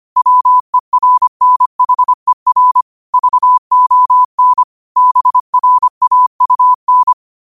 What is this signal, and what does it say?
Morse code, heard as a keyed tone, about 25 wpm, 1000 Hz. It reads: WERNHER VON BRAUN